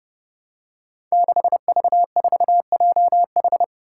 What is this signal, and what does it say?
Morse code, heard as a keyed tone, about 29 wpm, 700 Hz. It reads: 6V4JH
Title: Five Challenging Letter-Number Combinations 30wpm